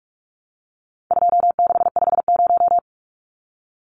Morse code, heard as a keyed tone, about 44 wpm, 700 Hz. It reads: 2650
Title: Numbers - 4-Digit 45wpm